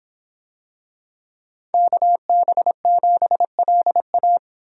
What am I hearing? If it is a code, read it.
Morse code, heard as a keyed tone, about 24 wpm, 700 Hz. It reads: KB7LA